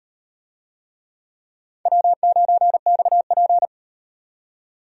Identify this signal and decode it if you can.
Morse code, heard as a keyed tone, about 38 wpm, 700 Hz. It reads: W9XP